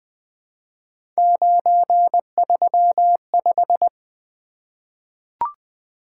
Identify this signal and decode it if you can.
Morse code, heard as a keyed tone, about 20 wpm, 700 Hz. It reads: 935
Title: Three Letter-Number Combinations - 20wpm